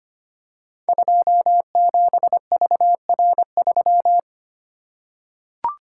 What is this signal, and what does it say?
Morse code, heard as a keyed tone, about 25 wpm, 700 Hz. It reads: 27VR3
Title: Code Groups - 5 Letter:Numbers - 25wpm